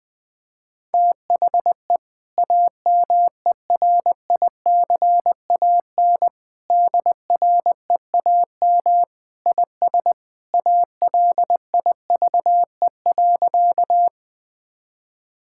Morse code, heard as a keyed tone, about 20 wpm, 700 Hz. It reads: THE AMERICAN DREAM IS ALIVE.
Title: Sentences with 2-Word Phrases - 20wpm